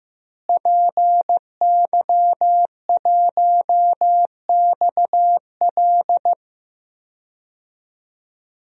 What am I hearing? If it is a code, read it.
Morse code, heard as a keyed tone, about 16 wpm, 700 Hz. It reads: PY1XL
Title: DXCC-100 Call Signs - 5 Characters - Repeated 2x - 15wpm